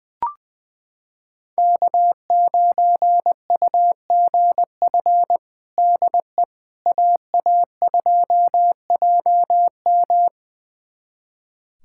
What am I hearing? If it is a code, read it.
Morse code, heard as a keyed tone, about 21 wpm, 700 Hz. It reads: K9UGF DE AA2JM
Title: USA Callsign DE Callsign - 5 Characters - 20wpm